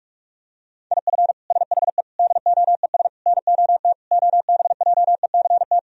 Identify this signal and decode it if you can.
Morse code, heard as a keyed tone, about 47 wpm, 700 Hz. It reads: IF SHE DOES NOT OBJECT
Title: Sentences from Top 1000 Words - Review Part 3 - 45wpm